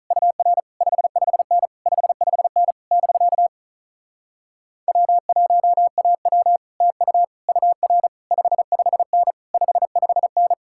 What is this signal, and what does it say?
Morse code, heard as a keyed tone, about 46 wpm, 700 Hz. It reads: UR 55N 55N <BK> W1AW TU UR 55N 55N